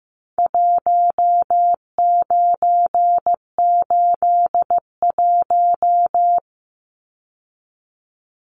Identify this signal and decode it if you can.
Morse code, heard as a keyed tone, about 15 wpm, 700 Hz. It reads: 1981
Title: Numbers - Contest Style - 4-Digit 15wpm